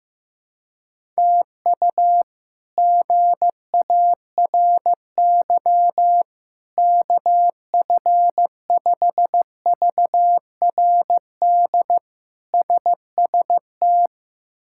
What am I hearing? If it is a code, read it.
Morse code, heard as a keyed tone, about 15 wpm, 700 Hz. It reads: TU GARY KF5VRD SST
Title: K1USN SST QSO Practice - 15wpm